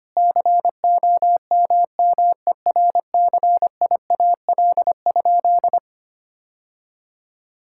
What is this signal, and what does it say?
Morse code, heard as a keyed tone, about 25 wpm, 700 Hz. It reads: COMMERCIAL?
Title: Sentences from Top 1000 Words - Review Part 5 - 25wpm